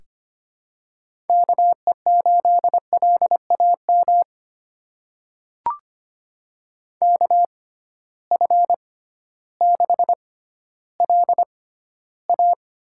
Morse code, heard as a keyed tone, about 25 wpm, spaced as for 13 wpm, 700 Hz. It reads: KE8LAM KF6LA
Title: USA Call Signs - 6 Characters - 6x Character Spacing - 25wpm